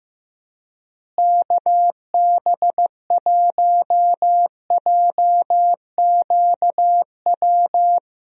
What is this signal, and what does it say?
Morse code, heard as a keyed tone, about 15 wpm, 700 Hz. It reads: KB1JQW
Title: USA Call Signs - 6 Characters - Repeated 3x - 15wpm